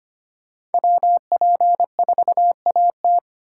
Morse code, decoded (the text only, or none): WP4AT